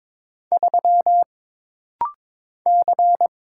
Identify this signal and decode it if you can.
Morse code, heard as a keyed tone, about 22 wpm, 700 Hz. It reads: 3 C
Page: This rapid-fire timing is geared to push your ICR (Instant Character Recognition) skills.